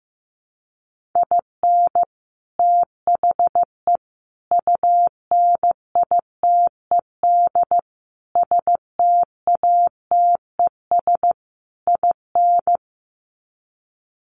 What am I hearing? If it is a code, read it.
Morse code, heard as a keyed tone, about 14 wpm, 700 Hz. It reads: IN THE UNITED STATES IN
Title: Sets of 5 Words 15wpm